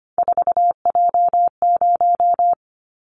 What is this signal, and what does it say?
Morse code, heard as a keyed tone, about 25 wpm, 700 Hz. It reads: 4J0
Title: Callsign Prefixes for Top 100 DXCC - 25wpm